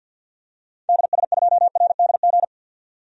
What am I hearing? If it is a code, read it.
Morse code, heard as a keyed tone, about 50 wpm, 700 Hz. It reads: DS2RDG
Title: DXCC-100 Call Signs - 6 Characters - 50wpm